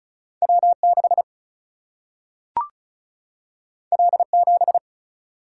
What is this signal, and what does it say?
Morse code, heard as a keyed tone, about 36 wpm, 700 Hz. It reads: W6 L7
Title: Single-Letter Single-Number 35wpm